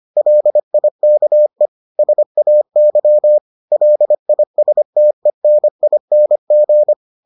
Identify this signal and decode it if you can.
Morse code, heard as a keyed tone, about 25 wpm, 600 Hz. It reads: LIKE SAY LISTENING